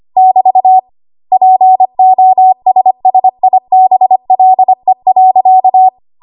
Morse code, heard as a keyed tone, about 25 wpm, 750 Hz. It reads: <BT> POSSIBLE.
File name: Archive / 25 WPM / 25WPM